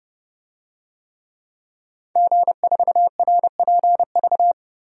Morse code, heard as a keyed tone, about 32 wpm, 700 Hz. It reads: G4RPV
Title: DXCC-100 Call Signs - 5 Characters - Repeated 3x - 30wpm